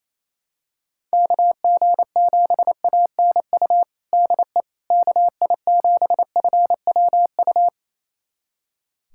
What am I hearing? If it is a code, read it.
Morse code, heard as a keyed tone, about 28 wpm, 700 Hz. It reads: KG7ANU DE KI7FWU
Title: USA Callsign DE Callsign - 6 Characters - 28wpm